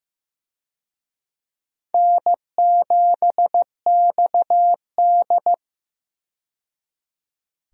Morse code, heard as a keyed tone, about 15 wpm, 700 Hz. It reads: N7XD
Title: USA Call Signs - 4 Characters - 15wpm